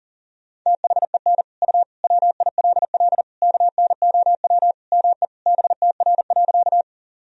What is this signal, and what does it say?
Morse code, heard as a keyed tone, about 40 wpm, 700 Hz. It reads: THEN U WILL KNOW ME BTR.
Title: Sentences from Top 400 Words and Abbreviations 40wpm